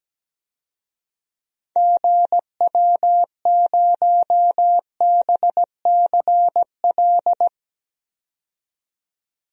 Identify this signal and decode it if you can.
Morse code, heard as a keyed tone, about 17 wpm, 700 Hz. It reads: GW0BCL
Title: DXCC-100 Call Signs - 6 Characters - 17wpm